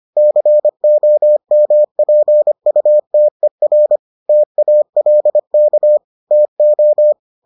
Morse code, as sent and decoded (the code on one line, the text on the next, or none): -.-. --- -- .--. ..- - . .-. / - .- .-.. -.- / - ---
COMPUTER TALK TO